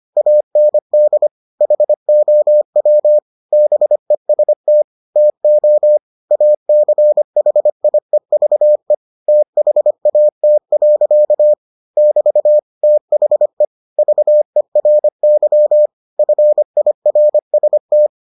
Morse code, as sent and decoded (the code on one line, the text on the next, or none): .- -. -.. / .... --- .-- / -... ... - / - --- / .- -.-. .... .. ...- . / - .... .- - .-.-.- / -...- / - .... / ...- . .-. -.-- / ..-. .. .-. ... -
AND HOW BEST TO ACHIEVE THAT. <BT> THE VERY FIRST